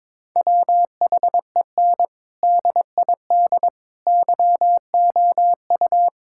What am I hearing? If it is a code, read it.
Morse code, heard as a keyed tone, about 22 wpm, 700 Hz. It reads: WHEN DID YOU
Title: Sentences from Top 1000 Words - Review Part 6 - 22wpm